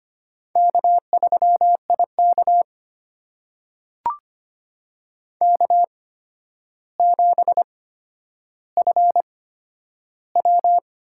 Morse code, heard as a keyed tone, about 25 wpm, spaced as for 11 wpm, 700 Hz. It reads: K3IK K7FW
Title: USA Call Signs - 4 Characters - 8x Character Spacing - 25wpm